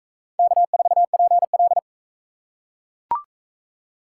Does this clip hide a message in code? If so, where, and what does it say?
Yes: Morse code from 0.4 to 1.8 s, heard as a keyed tone, about 41 wpm, 700 Hz. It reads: K4PL